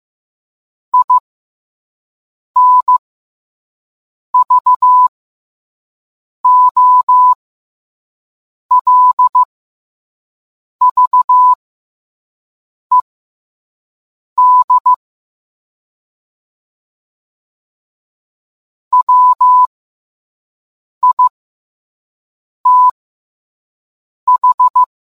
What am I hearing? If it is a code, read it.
Morse code, heard as a keyed tone, about 15 wpm, spaced as for 5 wpm, 1000 Hz. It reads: INVOLVED WITH